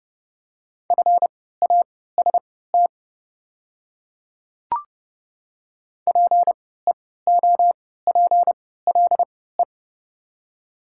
Morse code, heard as a keyed tone, about 30 wpm, spaced as for 17 wpm, 700 Hz. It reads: FAST PEOPLE
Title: 3x Character Spacing - Top 500 Words - 30wpm